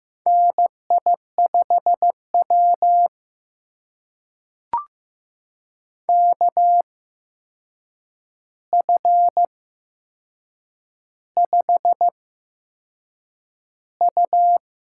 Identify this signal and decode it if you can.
Morse code, heard as a keyed tone, about 15 wpm, spaced as for 6 wpm, 700 Hz. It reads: NI5W KF5U
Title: USA Call Signs - 8x Character Spacing - 15wpm